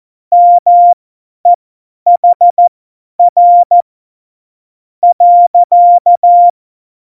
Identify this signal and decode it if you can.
Morse code, heard as a keyed tone, about 14 wpm, spaced as for 8 wpm, 700 Hz. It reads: MEHR.